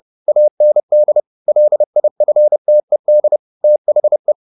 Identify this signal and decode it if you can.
Morse code, heard as a keyed tone, about 30 wpm, 600 Hz. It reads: AND LIFTED THE
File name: the old man and the sea / the old man and the sea 1 30wpm